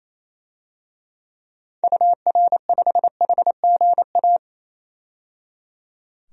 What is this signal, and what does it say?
Morse code, heard as a keyed tone, about 28 wpm, 700 Hz. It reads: UR5HGA
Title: DXCC-100 Call Signs - 6 Characters - 28wpm